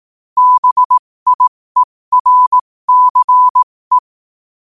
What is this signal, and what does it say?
Morse code, heard as a keyed tone, about 18 wpm, spaced as for 15 wpm, 1000 Hz. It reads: BIERCE
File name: qotdfiles / 15wpm